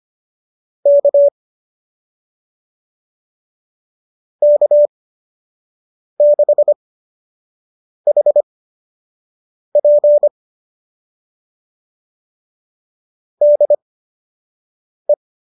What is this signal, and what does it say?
Morse code, heard as a keyed tone, about 25 wpm, spaced as for 6 wpm, 590 Hz. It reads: K K6HP DE
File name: qso003 06